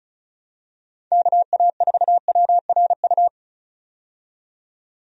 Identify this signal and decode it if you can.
Morse code, heard as a keyed tone, about 36 wpm, 700 Hz. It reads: KA4WRU
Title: USA Call Signs - 6 Characters - 35wpm